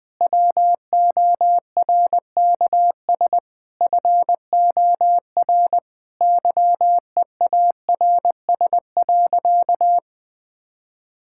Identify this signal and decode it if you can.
Morse code, heard as a keyed tone, about 20 wpm, 700 Hz. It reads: WORKS FOR YEARS.
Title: Sentences with 6-Word Phrases - Spoken Beginning - 20wpm